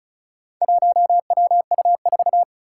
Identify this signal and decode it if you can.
Morse code, heard as a keyed tone, about 36 wpm, 700 Hz. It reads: 1WU4
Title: Four Challenging Letter-Number Combinations 35wpm